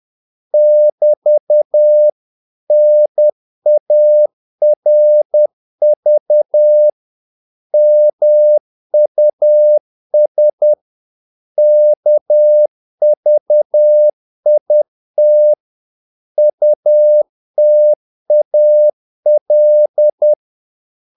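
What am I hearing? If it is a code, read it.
Morse code, heard as a keyed tone, about 10 wpm, 600 Hz. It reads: <BT> NARV MUS KVIT UTAL
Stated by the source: Morse practice of short Danish words at 10 wpm, from CW med Gnister.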